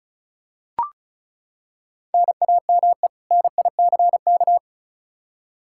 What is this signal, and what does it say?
Morse code, heard as a keyed tone, about 34 wpm, 700 Hz. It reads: NAME NICK
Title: Practice QSO - Exchange 1 - Beginning - 35wpm